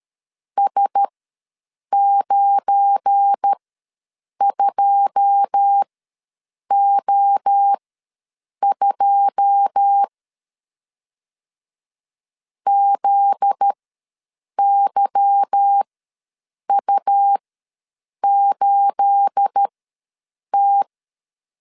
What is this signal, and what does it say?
Morse code, heard as a keyed tone, about 13 wpm, spaced as for 7 wpm, 800 Hz. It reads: S92O2 ZYU8T
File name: LEZIONI / 9 28 ws 12wpm 3354